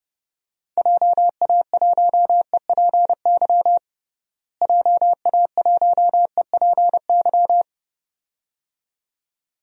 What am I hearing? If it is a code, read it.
Morse code, heard as a keyed tone, about 32 wpm, 700 Hz. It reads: JA1EPY JA1EPY
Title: DXCC-100 Call Signs - 6 Characters - Repeated 2x - 30wpm